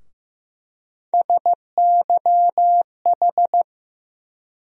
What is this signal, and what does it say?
Morse code, heard as a keyed tone, about 15 wpm, 700 Hz. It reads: SYH
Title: USA Call Sign Postfixes - 3 Letters 15wpm